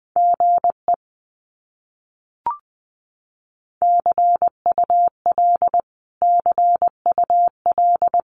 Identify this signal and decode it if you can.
Morse code, heard as a keyed tone, about 20 wpm, 700 Hz. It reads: GE CUL CUL
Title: Abbreviations Repeated 3x - 20wpm